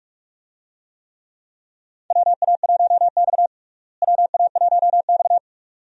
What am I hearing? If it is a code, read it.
Morse code, heard as a keyed tone, about 43 wpm, 700 Hz. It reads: WA1X WA1X
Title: USA Call Signs - 4 Characters - Repeated 2x - 45wpm